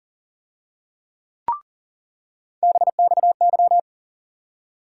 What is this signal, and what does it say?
Morse code, heard as a keyed tone, about 40 wpm, 700 Hz. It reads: BXY
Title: USA Call Sign Postfixes - 3 Letters 40wpm